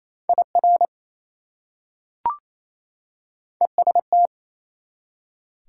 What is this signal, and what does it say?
Morse code, heard as a keyed tone, about 28 wpm, 700 Hz. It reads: IR EST
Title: Prefixes and Suffixes 28wpm